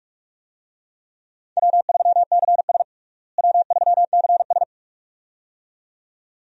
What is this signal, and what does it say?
Morse code, heard as a keyed tone, about 42 wpm, 700 Hz. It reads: W3CS W3CS
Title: USA Call Signs - 4 Characters - Repeated 2x - 45wpm